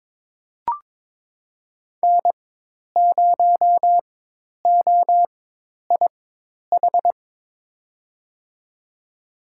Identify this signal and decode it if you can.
Morse code, heard as a keyed tone, about 23 wpm, spaced as for 10 wpm, 700 Hz. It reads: N0OIH
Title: USA Call Signs - 5 Characters - 4x Character Spacing - 22wpm